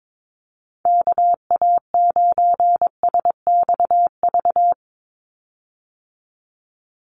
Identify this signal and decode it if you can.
Morse code, heard as a keyed tone, about 22 wpm, 700 Hz. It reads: KA9SXV
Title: Call Signs - 22wpm